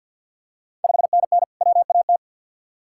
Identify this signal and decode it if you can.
Morse code, heard as a keyed tone, about 51 wpm, 690 Hz. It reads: 5NN WAT
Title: Ontario QSO Party Exchange 50wpm